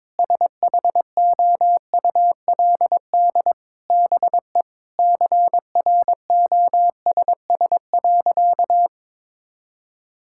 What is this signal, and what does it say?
Morse code, heard as a keyed tone, about 21 wpm, 700 Hz. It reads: SHOULD BE CROSS.
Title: Sentences from Top 1000 Words - Review Part 6 - 22wpm